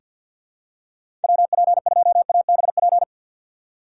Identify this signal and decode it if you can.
Morse code, heard as a keyed tone, about 48 wpm, 700 Hz. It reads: WP2ABP
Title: USA Call Signs - 6 Characters - 6x Character Spacing - 50wpm